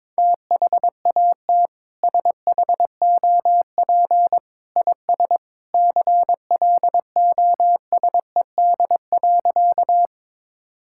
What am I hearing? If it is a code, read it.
Morse code, heard as a keyed tone, about 22 wpm, 700 Hz. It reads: THAT SHOP IS CLOSED.
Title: Sentences from Top 1600 Words 22wpm